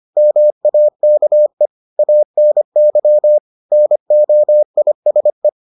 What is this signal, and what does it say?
Morse code, heard as a keyed tone, about 25 wpm, 600 Hz. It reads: MAKE ANY NOISE